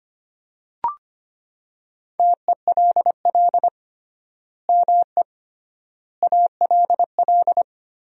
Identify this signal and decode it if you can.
Morse code, heard as a keyed tone, about 25 wpm, 700 Hz. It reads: TELL ME ALL
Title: 3x Word-Spacing - Sentences from Top 200 Words - 25wpm